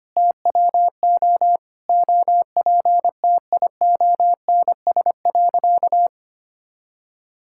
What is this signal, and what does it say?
Morse code, heard as a keyed tone, about 25 wpm, 700 Hz. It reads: TWO OPTIONS.
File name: practice / 25wpm / Sentences with 3-Word Phrases - 25wpm